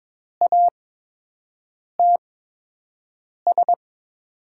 Morse code, heard as a keyed tone, about 22 wpm, spaced as for 6 wpm, 700 Hz. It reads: ATS
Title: USA Call Signs - 8x Character Spacing - 22wpm